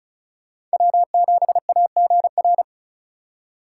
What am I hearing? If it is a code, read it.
Morse code, heard as a keyed tone, about 36 wpm, 700 Hz. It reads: W7AGR
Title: USA Call Signs - 5 Characters - 4x Character Spacing - 35wpm